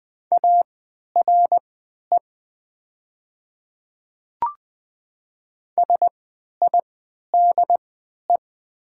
Morse code, heard as a keyed tone, about 20 wpm, spaced as for 11 wpm, 700 Hz. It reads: ARE SIDE
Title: 3x Character Spacing - Top 100 Words - 20wpm